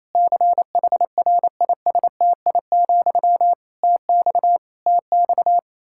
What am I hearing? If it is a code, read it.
Morse code, heard as a keyed tone, about 28 wpm, 700 Hz. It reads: CHRISTI, TX TX
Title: K1USN SST QSO Practice 28wpm